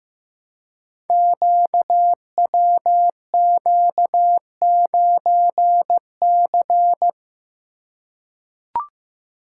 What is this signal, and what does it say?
Morse code, heard as a keyed tone, about 15 wpm, 700 Hz. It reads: QWQ9C
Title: Code Groups - 5 Letter:Numbers - 15wpm